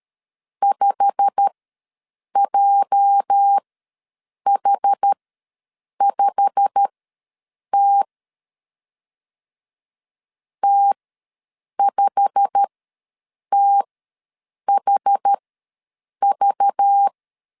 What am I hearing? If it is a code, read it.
Morse code, heard as a keyed tone, about 13 wpm, spaced as for 7 wpm, 800 Hz. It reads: 5JH5T T5THV
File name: LEZIONI / 8 YJ45 / 8 YJ45 ws 12wpm 3146